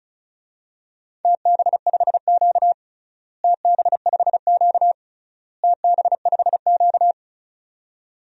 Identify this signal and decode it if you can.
Morse code, heard as a keyed tone, about 34 wpm, 700 Hz. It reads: TB5Q TB5Q TB5Q
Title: DXCC-100 Call Signs - 4 Characters - Repeated 3x - 35wpm